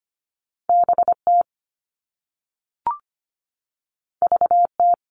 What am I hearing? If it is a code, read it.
Morse code, heard as a keyed tone, about 25 wpm, 700 Hz. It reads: BT VT
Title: All Cut Numbers - 2-Digit 25wpm